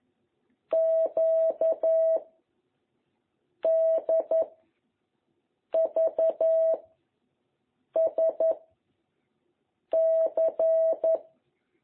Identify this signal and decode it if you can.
Morse code, heard as a keyed tone, about 11 wpm, 660 Hz. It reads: Q D V S C